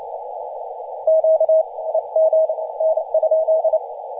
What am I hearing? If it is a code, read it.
Morse code, heard as a keyed tone, about 28 wpm, 640 Hz, with radio noise. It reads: Q EM N?
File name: e51q-14-034mhz-cw